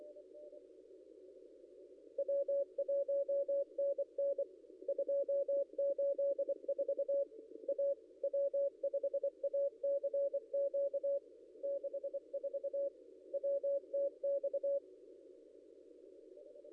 Morse code, heard as a keyed tone, about 24 wpm, 560 Hz. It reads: W1NN 284 A W5ACQ 64 WTX
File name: contests / ss-cw-k5tr-op-n5rz -0700